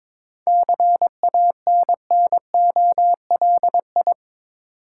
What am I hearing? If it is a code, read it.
Morse code, heard as a keyed tone, about 22 wpm, 690 Hz. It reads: CANNOLI
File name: Difficult Words with Hints - 22wpm